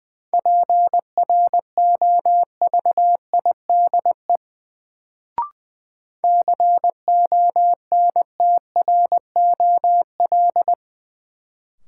Morse code, heard as a keyed tone, about 20 wpm, 700 Hz. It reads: PROVIDE CONTROL
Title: Sentences from Top 1000 Words - Review Part 2 - New Word Drill - 20wpm